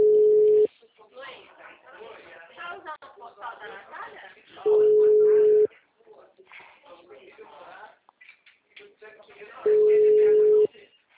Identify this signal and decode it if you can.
Ringback tone (Europe, 425 Hz), with speech.